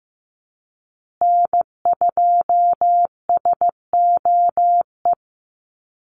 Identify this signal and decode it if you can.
Morse code, heard as a keyed tone, about 15 wpm, 700 Hz. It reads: N2SOE
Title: USA Call Signs Repeated 2x 15wpm